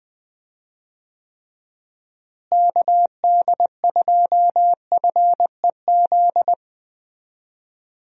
Morse code, heard as a keyed tone, about 20 wpm, 700 Hz. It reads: KD2FEZ
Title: USA Call Signs - 6 Characters - Repeated 2x - 20wpm